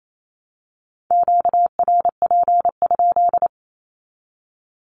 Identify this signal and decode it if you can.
Morse code, heard as a keyed tone, about 28 wpm, 700 Hz. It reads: QRP?
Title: Q-Codes 28wpm